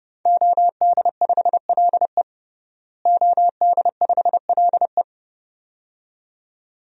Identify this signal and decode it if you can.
Morse code, heard as a keyed tone, about 30 wpm, 700 Hz. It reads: OD5LE OD5LE
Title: DXCC-100 Call Signs - 5 Characters - Repeated 2x - 30wpm